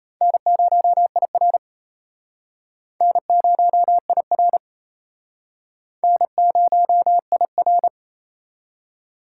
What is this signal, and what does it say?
Morse code, heard as a keyed tone, about 32 wpm, 700 Hz. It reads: N0IR N0IR N0IR
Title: Speed Racing - USA 1x2 Call Signs 28wpm